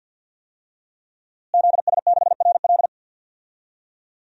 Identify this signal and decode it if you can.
Morse code, heard as a keyed tone, about 52 wpm, 700 Hz. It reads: ZS6RL
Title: DXCC-100 Call Signs - 5 Characters - Repeated 3x - 50wpm